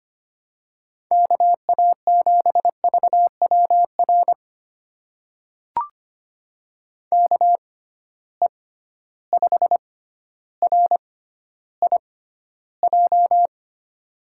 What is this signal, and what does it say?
Morse code, heard as a keyed tone, about 25 wpm, spaced as for 13 wpm, 700 Hz. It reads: KA7VWR KE5RIJ
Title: USA Call Signs - 6 Characters - 6x Character Spacing - 25wpm